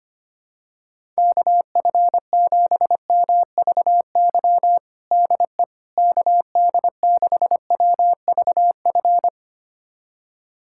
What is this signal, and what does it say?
Morse code, heard as a keyed tone, about 25 wpm, 700 Hz. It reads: KF7MVY DE KD6WVF
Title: USA Callsign DE Callsign - 6 Characters - 25wpm